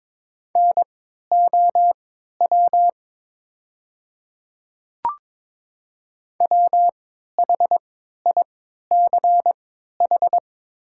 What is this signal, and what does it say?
Morse code, heard as a keyed tone, about 22 wpm, spaced as for 12 wpm, 700 Hz. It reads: NOW WHICH
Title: 3x Character Spacing - Top 100 Words - 22wpm